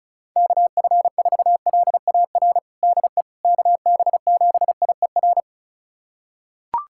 Morse code, heard as a keyed tone, about 36 wpm, 700 Hz. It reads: KF4LAR DE KB7IER
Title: USA Callsign DE Callsign - 6 Characters - 35wpm